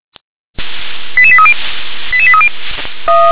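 Five-tone selective call (ZVEI), heard as quick runs of five tones, tones of about 70 ms. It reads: 88830 88830
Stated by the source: ZVEI five-tone selective call received by SDR.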